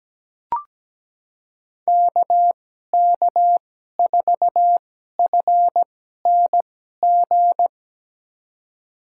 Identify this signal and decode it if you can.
Morse code, heard as a keyed tone, about 17 wpm, spaced as for 12 wpm, 700 Hz. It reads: KK4FNG
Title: USA Call Signs - 6 Characters - 2x Character Spacing - 17wpm